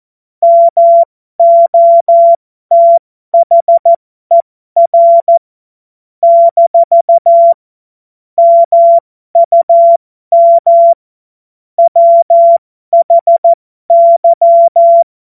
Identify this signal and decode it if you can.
Morse code, heard as a keyed tone, about 14 wpm, spaced as for 12 wpm, 680 Hz. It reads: MOTHER - MUM WHY
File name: Morsekurs / Witze